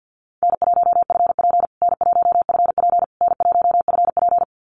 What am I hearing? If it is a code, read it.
Morse code, heard as a keyed tone, about 50 wpm, 700 Hz. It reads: N1FP N1FP N1FP
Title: USA Call Signs Repeated 3x 50wpm